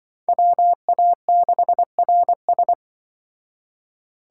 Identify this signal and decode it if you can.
Morse code, heard as a keyed tone, about 25 wpm, 700 Hz. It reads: WA6RS
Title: Speed Racing - USA 2x2 Call Signs 20wpm